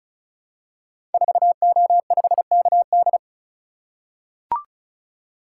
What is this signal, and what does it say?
Morse code, heard as a keyed tone, about 36 wpm, 700 Hz. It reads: 4O5KD
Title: Code Groups - 5 Letter:Numbers - 35wpm